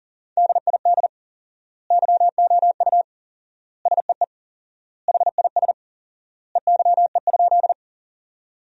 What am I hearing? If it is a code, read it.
Morse code, heard as a keyed tone, about 40 wpm, 700 Hz. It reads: DID YOU SEE HIS EYE?
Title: 4x Word-Spacing - Sentences from Top 300 Words - 40wpm